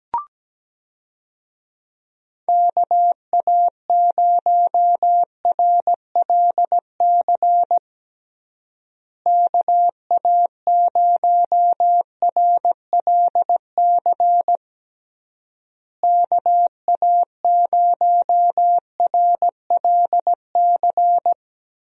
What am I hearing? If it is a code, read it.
Morse code, heard as a keyed tone, about 17 wpm, 700 Hz. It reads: KA0RLC KA0RLC KA0RLC
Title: USA Call Signs - 6 Characters - Repeated 3x - 17wpm